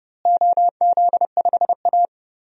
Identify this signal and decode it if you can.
Morse code, heard as a keyed tone, about 30 wpm, 700 Hz. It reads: OZ5A
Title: DXCC-100 Call Signs - 4 Characters - Repeated 3x - 30wpm